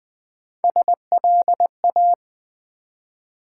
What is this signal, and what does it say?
Morse code, heard as a keyed tone, about 20 wpm, 700 Hz. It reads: SLA
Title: Build A Word - 1 Letter At A Time - 5-Letter Words - 20wpm